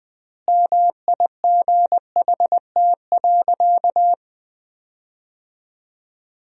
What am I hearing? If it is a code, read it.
Morse code, heard as a keyed tone, about 20 wpm, 700 Hz. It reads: MIGHT.
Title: 3x Word-Spacing - Sentences from Top 500 Words - 20wpm